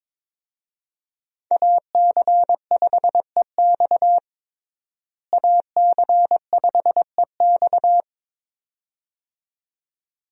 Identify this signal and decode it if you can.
Morse code, heard as a keyed tone, about 21 wpm, 700 Hz. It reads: AC5EX AC5EX
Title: USA Call Signs - 5 Characters - Repeated 2x - 22wpm